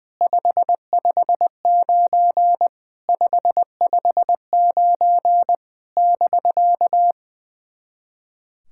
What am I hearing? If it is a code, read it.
Morse code, heard as a keyed tone, about 20 wpm, 690 Hz. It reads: 559 559 <BK>
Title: SOTA QSO Practice 20wpm